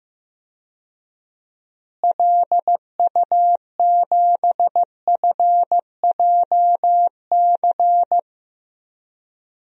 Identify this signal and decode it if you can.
Morse code, heard as a keyed tone, about 15 wpm, 700 Hz. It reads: LU7FJC
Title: DXCC-100 Call Signs - 6 Characters - Repeated 3x - 15wpm